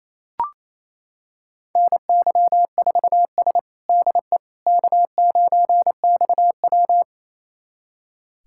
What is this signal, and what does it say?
Morse code, heard as a keyed tone, about 27 wpm, 700 Hz. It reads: NY4S DE K9XW
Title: USA Callsign DE Callsign - 4 Characters - 28wpm